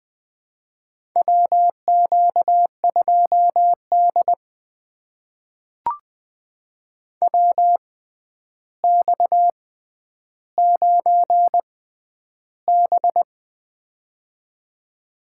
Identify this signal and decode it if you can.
Morse code, heard as a keyed tone, about 20 wpm, spaced as for 10 wpm, 700 Hz. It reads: WQ2D WX9B
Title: USA Call Signs - 4 Characters - 6x Character Spacing - 20wpm